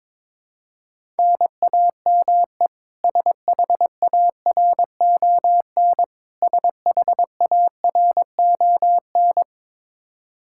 Morse code, heard as a keyed tone, about 22 wpm, 700 Hz. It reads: NAME SHARON SHARON
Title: Practice QSO - Exchange 1 - Beginning - 22wpm